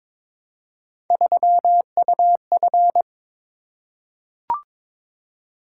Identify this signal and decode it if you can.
Morse code, heard as a keyed tone, about 22 wpm, 700 Hz. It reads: 3UF
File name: practice / 22wpm / Three Letter-Number Combinations - 22wpm